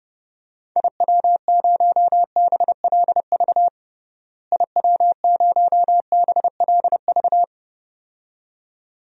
Morse code, heard as a keyed tone, about 30 wpm, 700 Hz. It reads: IW0BLV IW0BLV
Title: DXCC-100 Call Signs - 6 Characters - Repeated 2x - 30wpm